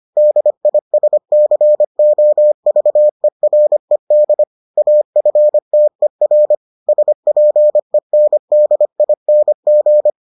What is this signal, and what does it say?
Morse code, heard as a keyed tone, about 25 wpm, 600 Hz. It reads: DISCOVERED AFTER SPENDING